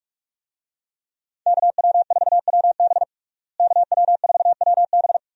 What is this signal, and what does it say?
Morse code, heard as a keyed tone, about 46 wpm, 700 Hz. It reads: KW4WB KW4WB
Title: USA Call Signs - 5 Characters - Repeated 2x - 45wpm